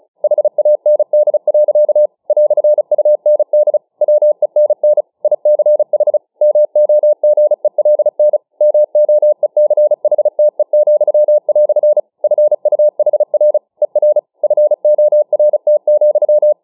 Morse code, heard as a keyed tone, about 36 wpm, 600 Hz. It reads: HAND. "UND WENN ICH MOGELN MOECHTE," FUHR ER FORT,